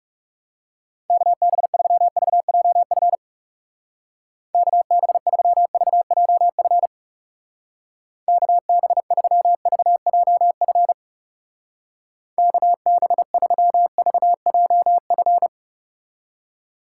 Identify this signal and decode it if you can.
Morse code, heard as a keyed tone, about 37 wpm, 700 Hz. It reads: KB3VJF KB3VJF KB3VJF KB3VJF
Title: Speed Racing - USA 2x3 Call Signs 30wpm